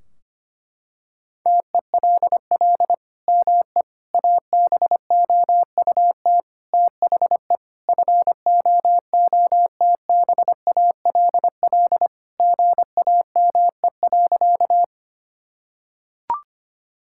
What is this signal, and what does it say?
Morse code, heard as a keyed tone, about 25 wpm, 700 Hz. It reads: TELL ME ABOUT THE FOOTBALL GAME.
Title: Sentences from Top 1600 Words 25wpm